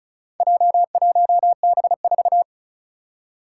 Morse code, heard as a keyed tone, about 36 wpm, 700 Hz. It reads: J1B4
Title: Four Challenging Letter-Number Combinations 35wpm